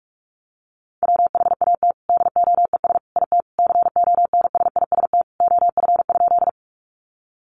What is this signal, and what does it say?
Morse code, heard as a keyed tone, about 44 wpm, 700 Hz. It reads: WHAT DOES IT CONSIST OF?
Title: Sentences from Top 1400 Words 45wpm